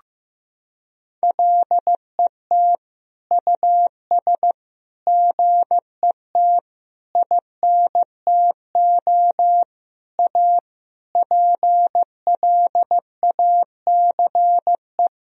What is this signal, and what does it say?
Morse code, heard as a keyed tone, about 15 wpm, 700 Hz. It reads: LET US GET INTO A PLACE
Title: Sentences from Top 1000 Words - Review Part 4 - 15wpm